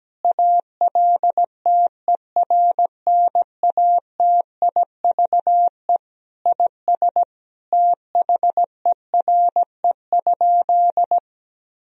Morse code, heard as a keyed tone, about 17 wpm, 700 Hz. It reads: ALTERNATIVE IS THERE?
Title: Sentences from Top 1000 Words - Review Part 4 - 17wpm